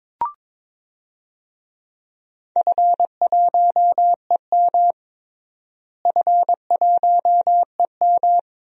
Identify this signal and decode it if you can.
Morse code, heard as a keyed tone, about 22 wpm, 700 Hz. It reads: F1EM F1EM
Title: DXCC-100 Call Signs - 4 Characters - Repeated 3x - 22wpm